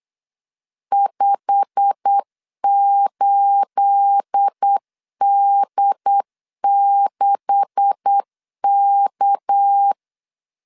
Morse code, heard as a keyed tone, about 8 wpm, 800 Hz. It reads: 58D6K
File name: CW / DET / 1 9 1500C S C 8wpm 5653